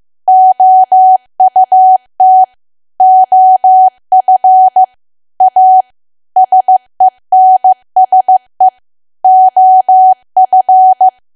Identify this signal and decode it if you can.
Morse code, heard as a keyed tone, about 15 wpm, 750 Hz. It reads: OUT OF A SENSE OF
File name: cwa / W1AW-Level2 / 15WPM